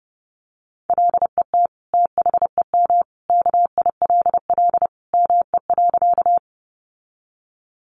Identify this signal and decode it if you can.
Morse code, heard as a keyed tone, about 30 wpm, 700 Hz. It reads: LET THEM KILL ME.
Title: Sentences from Top 1400 Words 30wpm